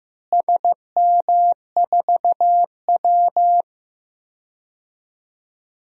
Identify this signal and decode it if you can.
Morse code, heard as a keyed tone, about 15 wpm, 700 Hz. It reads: SM4W
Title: DXCC-100 Call Signs - 4 Characters - Repeated 2x - 15wpm